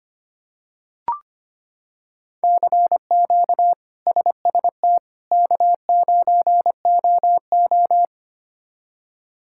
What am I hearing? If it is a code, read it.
Morse code, heard as a keyed tone, about 25 wpm, 700 Hz. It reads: CQ SST K9OO
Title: K1USN SST QSO Practice - 25wpm